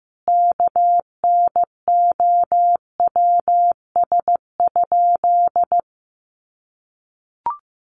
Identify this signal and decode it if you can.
Morse code, heard as a keyed tone, about 15 wpm, 700 Hz. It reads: KNOWS?
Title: A Sound Of Thunder 15wpm